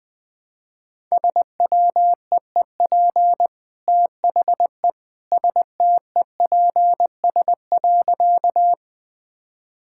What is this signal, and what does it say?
Morse code, heard as a keyed tone, about 20 wpm, 700 Hz. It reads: SWEEP THE STEPS.
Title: Easy Sentences with Hints - 20wpm